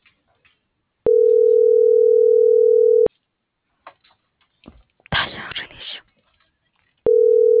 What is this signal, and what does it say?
Ringback tone (North America), 440 + 480 Hz.